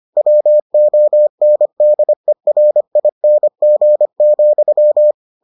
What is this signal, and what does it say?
Morse code, heard as a keyed tone, about 25 wpm, 600 Hz. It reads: WONDERING,